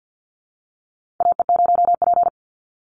Morse code, heard as a keyed tone, about 50 wpm, 700 Hz. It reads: AE0P
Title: Call Signs - 50wpm